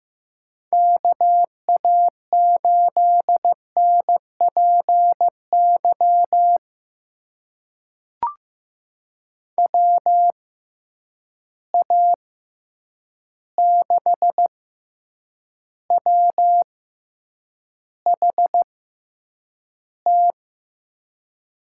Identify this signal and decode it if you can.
Morse code, heard as a keyed tone, about 15 wpm, spaced as for 8 wpm, 700 Hz. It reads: KA8NPY WA6WHT
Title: USA Call Signs - 6 Characters - 6x Character Spacing - 15wpm